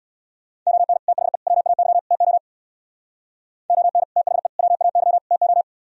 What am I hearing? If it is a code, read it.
Morse code, heard as a keyed tone, about 59 wpm, 700 Hz. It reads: ON THE WAY TO ON THE WAY TO
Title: Speed-Racing - Sets of 4 Words 50wpm